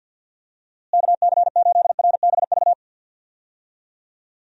Morse code, heard as a keyed tone, about 52 wpm, 700 Hz. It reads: KC8RBV
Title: USA Call Signs - 6 Characters - Repeated 2x - 50wpm